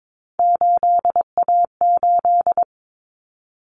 Morse code, heard as a keyed tone, about 22 wpm, 700 Hz. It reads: 8A8
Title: Common Cut Numbers - 3-Digit 22wpm